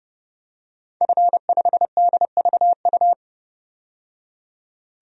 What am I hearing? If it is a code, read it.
Morse code, heard as a keyed tone, about 32 wpm, 700 Hz. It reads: F5DVU
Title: DXCC-100 Call Signs - 5 Characters - 30wpm